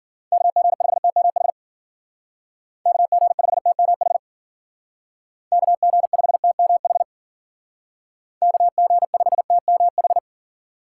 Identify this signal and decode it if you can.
Morse code, heard as a keyed tone, about 49 wpm, 700 Hz. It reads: KG5TMH KG5TMH KG5TMH KG5TMH
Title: Speed Racing - USA 2x3 Call Signs 40wpm